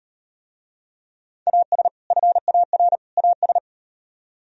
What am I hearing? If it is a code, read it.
Morse code, heard as a keyed tone, about 37 wpm, 700 Hz. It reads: AS FAR AS